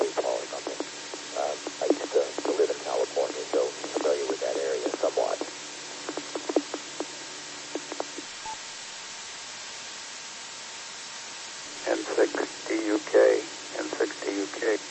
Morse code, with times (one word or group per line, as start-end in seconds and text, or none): none